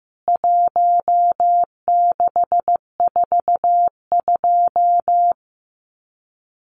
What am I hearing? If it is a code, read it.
Morse code, heard as a keyed tone, about 15 wpm, 700 Hz. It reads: 1642
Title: CWT Exchange 15wpm